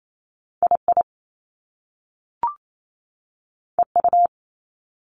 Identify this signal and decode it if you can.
Morse code, heard as a keyed tone, about 28 wpm, spaced as for 6 wpm, 700 Hz. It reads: I IEU
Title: Top Two Letter Combinations in Words 28wpm